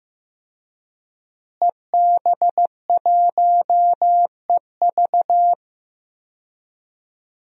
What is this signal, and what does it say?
Morse code, heard as a keyed tone, about 15 wpm, 700 Hz. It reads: EB1EV
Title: DXCC-100 Call Signs - 5 Characters - Repeated 2x - 15wpm